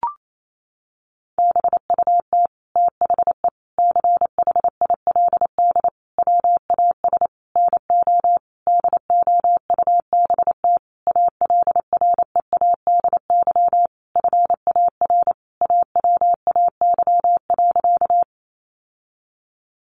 Morse code, heard as a keyed tone, about 28 wpm, 700 Hz. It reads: BUT THE CHILD WAS NO DOUBT ALREADY FAR AWAY.
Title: Sentences from Top 1500 Words 28wpm